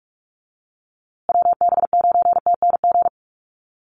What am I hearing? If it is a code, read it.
Morse code, heard as a keyed tone, about 44 wpm, 700 Hz. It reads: WB9TNG
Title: USA Call Signs Repeated 3x 45wpm